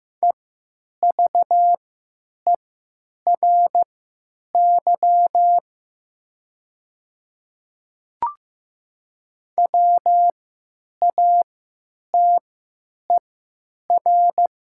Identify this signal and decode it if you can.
Morse code, heard as a keyed tone, about 15 wpm, spaced as for 9 wpm, 700 Hz. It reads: EVERY WATER
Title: 3x Character Spacing - Top 200 Words - 15wpm